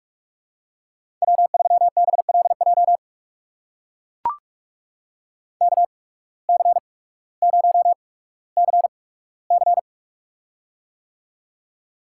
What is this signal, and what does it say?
Morse code, heard as a keyed tone, about 44 wpm, spaced as for 19 wpm, 700 Hz. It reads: W3BLJ KC0CC
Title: USA Call Signs - 5 Characters - 8x Character Spacing - 45wpm